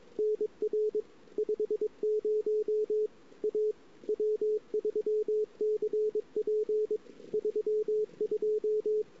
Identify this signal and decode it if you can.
Morse code, heard as a keyed tone, about 29 wpm, 410 Hz. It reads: NR 50 A W3CP 32
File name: CW / 2008-ss-cw-k5tr-op-n5rz -22:00